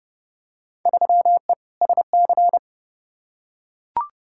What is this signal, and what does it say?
Morse code, heard as a keyed tone, about 30 wpm, 700 Hz. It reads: 3E SC
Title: Field Day Exchanges Encore - 30wpm